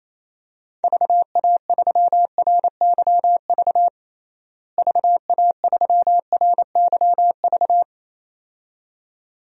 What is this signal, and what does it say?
Morse code, heard as a keyed tone, about 27 wpm, 700 Hz. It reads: VA3RYV VA3RYV
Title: DXCC-100 Call Signs - 6 Characters - Repeated 2x - 28wpm